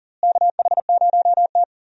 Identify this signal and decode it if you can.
Morse code, heard as a keyed tone, about 40 wpm, 700 Hz. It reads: KH0T
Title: USA Call Signs - 4 Characters - Repeated 3x - 40wpm